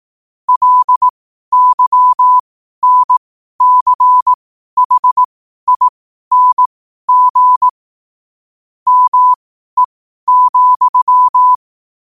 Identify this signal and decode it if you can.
Morse code, heard as a keyed tone, about 18 wpm, spaced as for 12 wpm, 1000 Hz. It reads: LYNCHING ME,
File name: qotdfiles / 12wpm